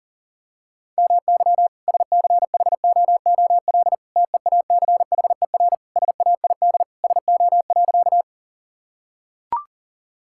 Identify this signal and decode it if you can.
Morse code, heard as a keyed tone, about 37 wpm, 700 Hz. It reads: MY SCHOOL TEACHER SAID SO.
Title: Sentences from Top 1000 Words - Review Part 6 - 40wpm